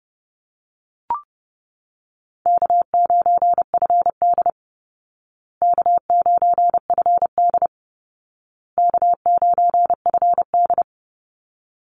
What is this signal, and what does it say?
Morse code, heard as a keyed tone, about 30 wpm, 700 Hz. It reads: K9FD K9FD K9FD
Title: Easy Call Signs Repeated 3x - 30wpm